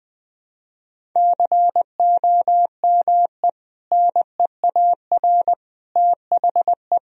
Morse code, heard as a keyed tone, about 20 wpm, 700 Hz. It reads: COME NEAR THE
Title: Sentences from Top 1000 Words - Review Part 1 - 20wpm